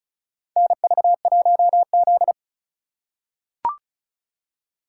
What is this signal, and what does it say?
Morse code, heard as a keyed tone, about 34 wpm, 700 Hz. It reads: NV1Z
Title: USA Call Signs - 4 Characters - 2x Character Spacing - 35wpm